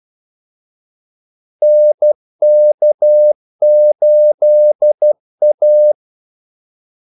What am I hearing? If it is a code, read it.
Morse code, heard as a keyed tone, about 12 wpm, 600 Hz. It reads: NK8A
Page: Kaldesignaler 12 wpm